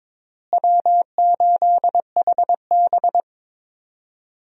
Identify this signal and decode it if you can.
Morse code, heard as a keyed tone, about 22 wpm, 700 Hz. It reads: W8HB
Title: USA Call Signs - 4 Characters - Repeated 3x - 22wpm